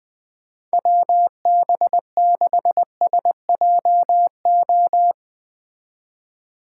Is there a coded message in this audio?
Yes: Morse code from 0.7 to 5.1 s, heard as a keyed tone, about 21 wpm, 700 Hz. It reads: WB6SJO